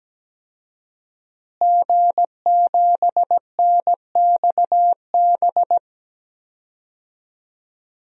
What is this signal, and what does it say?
Morse code, heard as a keyed tone, about 17 wpm, 700 Hz. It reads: G7NXB
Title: DXCC-100 Call Signs - 5 Characters - Repeated 2x - 17wpm